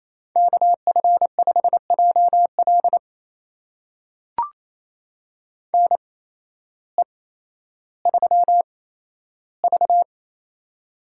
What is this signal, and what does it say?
Morse code, heard as a keyed tone, about 28 wpm, spaced as for 13 wpm, 700 Hz. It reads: KF5JL NE3V
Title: USA Call Signs - 8x Character Spacing - 28wpm